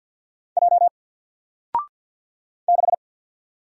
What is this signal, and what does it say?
Morse code, heard as a keyed tone, about 48 wpm, 700 Hz. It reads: J 6
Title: Single Character (Letter:Number:Punctuation:Prosign) Alt 50wpm